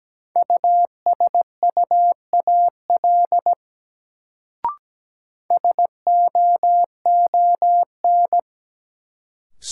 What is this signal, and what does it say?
Morse code, heard as a keyed tone, about 17 wpm, 700 Hz. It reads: USUAL SOON
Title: Sentences from Top 400 Words - New Word Drill - 17wpm